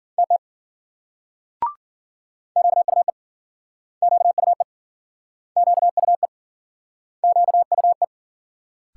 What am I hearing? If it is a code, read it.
Morse code, heard as a keyed tone, about 56 wpm, 700 Hz. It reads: TT QUE QUE QUE QUE